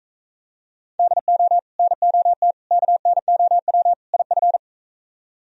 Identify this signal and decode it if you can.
Morse code, heard as a keyed tone, about 43 wpm, 700 Hz. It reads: DO NOT KNOW IF